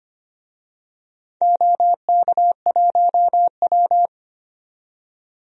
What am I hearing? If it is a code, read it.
Morse code, heard as a keyed tone, about 25 wpm, 700 Hz. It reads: OK1W
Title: DXCC-100 Call Signs - 4 Characters - 25wpm